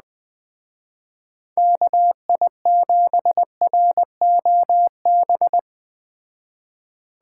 Morse code, heard as a keyed tone, about 20 wpm, 700 Hz. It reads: KI7ROB
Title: Speed Racing - USA 2x3 Call Signs 17wpm